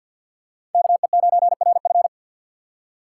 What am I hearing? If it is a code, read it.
Morse code, heard as a keyed tone, about 51 wpm, 700 Hz. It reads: KE9RF
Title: USA Call Signs - 3x Character Spacing - 50wpm